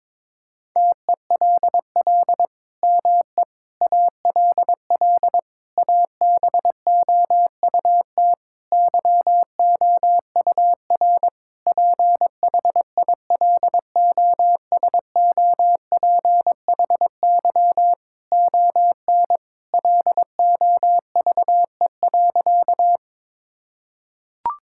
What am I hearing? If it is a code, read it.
Morse code, heard as a keyed tone, about 22 wpm, 700 Hz. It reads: TELL ME ALL ABOUT YOUR PHILOSOPHY ON LOVE.
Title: Sentences from Top 1000 Words - Review Part 6 - 22wpm